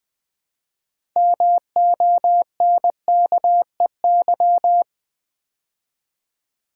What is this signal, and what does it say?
Morse code, heard as a keyed tone, about 20 wpm, 700 Hz. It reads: MONKEY
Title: Build A Word - 2 Letters At A Time - 6-Letter Words - 20wpm